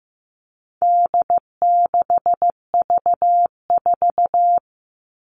Morse code, heard as a keyed tone, about 15 wpm, 700 Hz. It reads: D6V4
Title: All Cut Numbers - 4-Digit 15wpm